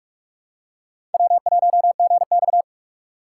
Morse code, heard as a keyed tone, about 47 wpm, 700 Hz. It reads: W1GX
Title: USA Call Signs - 4 Characters - 6x Character Spacing - 45wpm